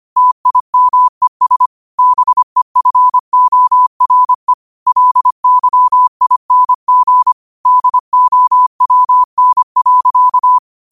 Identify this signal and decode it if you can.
Morse code, heard as a keyed tone, about 25 wpm, 1000 Hz. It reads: TIMES BEFORE LYING DOWN.
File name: qotdfiles / 25wpm